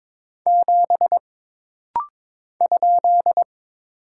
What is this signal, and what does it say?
Morse code, heard as a keyed tone, about 22 wpm, 700 Hz. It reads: 7 ?
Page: This rapid-fire timing is geared to push your ICR (Instant Character Recognition) skills.